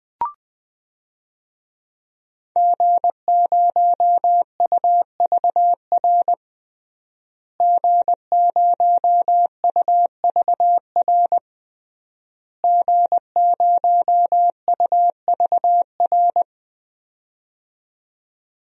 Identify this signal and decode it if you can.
Morse code, heard as a keyed tone, about 20 wpm, 700 Hz. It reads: G0UVR G0UVR G0UVR
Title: DXCC-100 Call Signs - 5 Characters - Repeated 3x - 20wpm